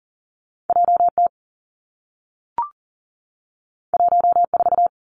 Morse code, heard as a keyed tone, about 38 wpm, 700 Hz. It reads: J T 1 4
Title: Two Challenging Letter-Number Combinations 40wpm